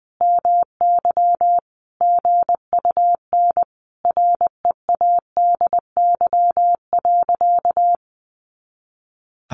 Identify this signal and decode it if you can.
Morse code, heard as a keyed tone, about 20 wpm, 700 Hz. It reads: MY GUN READY.
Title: A Sound Of Thunder 20wpm